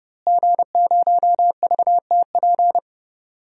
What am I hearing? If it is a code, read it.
Morse code, heard as a keyed tone, about 30 wpm, 700 Hz. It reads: G0VTP
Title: DXCC-100 Call Signs - 5 Characters - Repeated 3x - 30wpm